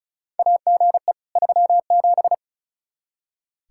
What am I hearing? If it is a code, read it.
Morse code, heard as a keyed tone, about 34 wpm, 700 Hz. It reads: AGE 37
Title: Practice QSO - Exchange 3 - More Info - 35wpm